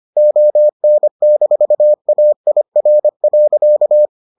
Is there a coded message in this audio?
Yes: Morse code from 0.2 to 4.1 s, heard as a keyed tone, about 25 wpm, 600 Hz. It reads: ON-AIR.